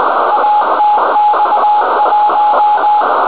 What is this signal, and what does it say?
Morse code, heard as a keyed tone, about 20 wpm, 840 Hz, with radio noise. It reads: ATX1